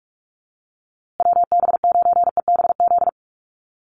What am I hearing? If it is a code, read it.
Morse code, heard as a keyed tone, about 44 wpm, 700 Hz. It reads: WB9EBZ
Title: USA Call Signs Repeated 3x 45wpm